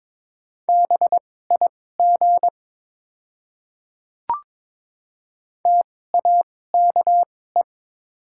Morse code, heard as a keyed tone, about 22 wpm, spaced as for 16 wpm, 700 Hz. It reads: BIG TAKE
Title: 2x Character Spacing - Top 300 Words - 22wpm